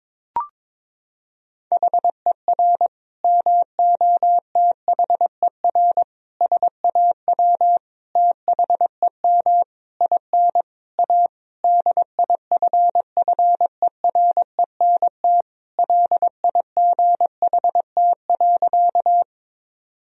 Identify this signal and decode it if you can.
Morse code, heard as a keyed tone, about 22 wpm, 700 Hz. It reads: HER MOTHER SAW THEM IN A DIFFERENT LIGHT.
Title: Sentences from Top 1000 Words - Review Part 3 - 22wpm